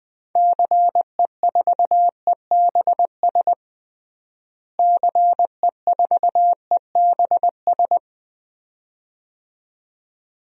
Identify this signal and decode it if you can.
Morse code, heard as a keyed tone, about 20 wpm, 700 Hz. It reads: CE4EBS CE4EBS
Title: DXCC-100 Call Signs - 6 Characters - Repeated 2x - 20wpm